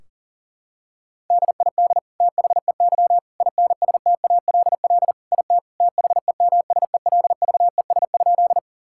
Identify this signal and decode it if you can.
Morse code, heard as a keyed tone, about 40 wpm, 700 Hz. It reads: DID THEY INSTALL IT THEMSELVES?
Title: Sentences from Top 1600 Words 40wpm